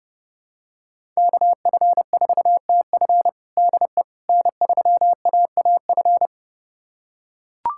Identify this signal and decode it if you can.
Morse code, heard as a keyed tone, about 30 wpm, 700 Hz. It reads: KF4TF DE N3AAF
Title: USA Callsign DE Callsign - 5 Characters - 30wpm